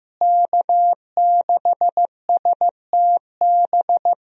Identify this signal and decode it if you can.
Morse code, heard as a keyed tone, about 15 wpm, 700 Hz. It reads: K6STB
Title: USA Call Signs - 5 Characters - Repeated 3x - 15wpm